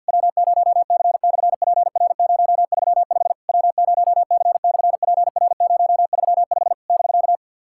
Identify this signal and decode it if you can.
Morse code, heard as a keyed tone, about 50 wpm, 690 Hz. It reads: W0C/PR035 W0C/PR035 <BK>
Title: SOTA QSO Practice 50wpm